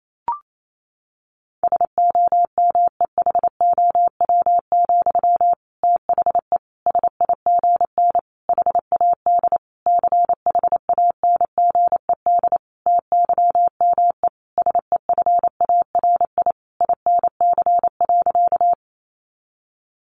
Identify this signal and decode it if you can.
Morse code, heard as a keyed tone, about 28 wpm, 700 Hz. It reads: SOMEHOW, THE SIGN HAD CHANGED TYME SEFARI INC.
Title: A Sound Of Thunder 28wpm